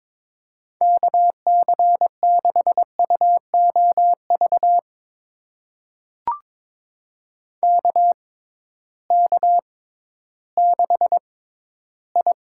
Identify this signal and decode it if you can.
Morse code, heard as a keyed tone, about 22 wpm, spaced as for 13 wpm, 700 Hz. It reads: KC6UOV KK6I
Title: USA Call Signs - 6 Characters - 6x Character Spacing - 22wpm